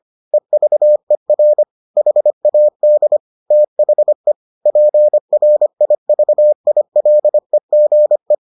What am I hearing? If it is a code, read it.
Morse code, heard as a keyed tone, about 25 wpm, 600 Hz. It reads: EVER HAD THE PRIVILEGE